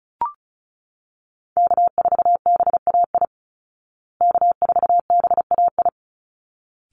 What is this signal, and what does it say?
Morse code, heard as a keyed tone, about 34 wpm, 700 Hz. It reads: K4BAI K4BAI
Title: Call Signs Repeated 3x - 35wpm